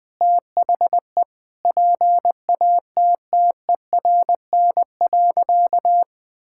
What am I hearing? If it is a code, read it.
Morse code, heard as a keyed tone, about 21 wpm, 700 Hz. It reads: THE PATTERN.
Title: Sentences with 5-Word Phrases - 20wpm